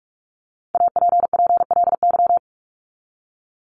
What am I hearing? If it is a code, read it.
Morse code, heard as a keyed tone, about 44 wpm, 700 Hz. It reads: APPLY
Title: ED Suffix - Top 100 Words with Base Words 45wpm